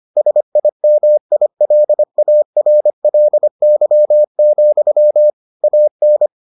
Morse code, heard as a keyed tone, about 25 wpm, 600 Hz. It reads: SIMILARLY, AN